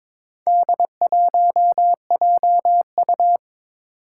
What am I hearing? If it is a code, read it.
Morse code, heard as a keyed tone, about 22 wpm, 700 Hz. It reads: D1JU
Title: Four Challenging Letter-Number Combinations 22wpm